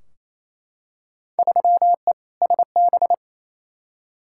Morse code, heard as a keyed tone, about 28 wpm, 700 Hz. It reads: 3E SB
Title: Field Day Exchanges Encore - 28wpm